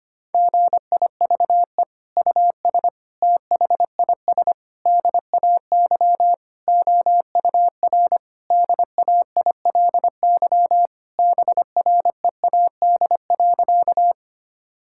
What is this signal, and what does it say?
Morse code, heard as a keyed tone, about 25 wpm, 700 Hz. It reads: GIVE US THIS DAY OUR DAILY BREAD.
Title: Sentences from Top 1600 Words 25wpm